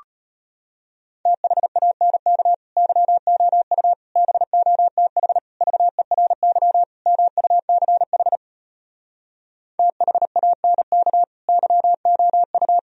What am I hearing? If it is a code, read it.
Morse code, heard as a keyed tone, about 33 wpm, 700 Hz. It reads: THANK YOU BOTH VERY MUCH THANK YOU